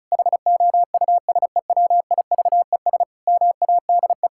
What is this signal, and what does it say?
Morse code, heard as a keyed tone, about 36 wpm, 700 Hz. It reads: HOUSEWIVES MADE
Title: Pangram Challenge v2 - 35wpm